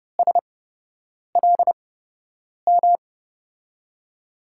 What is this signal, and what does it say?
Morse code, heard as a keyed tone, about 32 wpm, spaced as for 8 wpm, 700 Hz. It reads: SLM
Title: USA Call Signs - 6 Characters - 8x Character Spacing - 30wpm